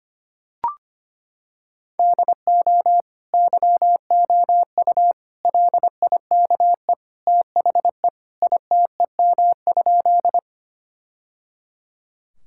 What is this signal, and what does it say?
Morse code, heard as a keyed tone, about 25 wpm, 700 Hz. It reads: DO YOU LIKE THE ITEM?
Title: Sentences from Top 1000 Words - Review Part 6 - 25wpm